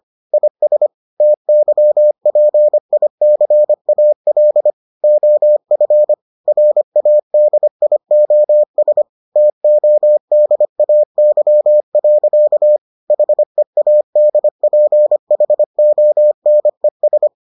Morse code, heard as a keyed tone, about 25 wpm, 600 Hz. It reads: IS TYPICAL OF RADIOS TODAY. HEADPHONES